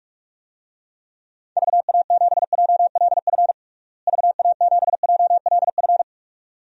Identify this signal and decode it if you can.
Morse code, heard as a keyed tone, about 46 wpm, 700 Hz. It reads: VA7JLF VA7JLF
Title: DXCC-100 Call Signs - 6 Characters - Repeated 2x - 45wpm